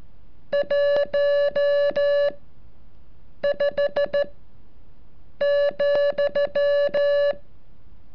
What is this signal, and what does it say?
Morse code, heard as a keyed tone, about 13 wpm, spaced as for 6 wpm, 570 Hz. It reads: 15,
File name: curso / ejercicio 36